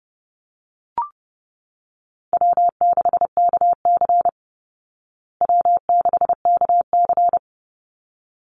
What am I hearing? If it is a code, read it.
Morse code, heard as a keyed tone, about 30 wpm, 700 Hz. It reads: W6KC W6KC
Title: Call Signs Repeated 3x - 30wpm